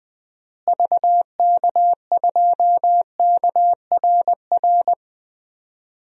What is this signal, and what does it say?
Morse code, heard as a keyed tone, about 20 wpm, 700 Hz. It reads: VK2KRR
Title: DXCC-100 Call Signs - 6 Characters - Repeated 2x - 20wpm